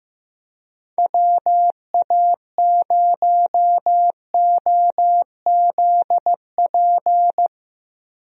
Morse code, heard as a keyed tone, about 16 wpm, 700 Hz. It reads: WA0OZP
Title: USA Call Signs - 6 Characters - Repeated 2x - 15wpm